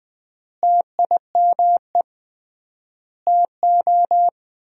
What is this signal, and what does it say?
Morse code, heard as a keyed tone, about 20 wpm, 700 Hz. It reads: TIME TO
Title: 3x Word-Spacing - Sentences from Top 200 Words - 20wpm